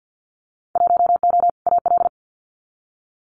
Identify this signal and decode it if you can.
Morse code, heard as a keyed tone, about 50 wpm, 700 Hz. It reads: 1O AL
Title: Winter Field Day Exchanges 50wpm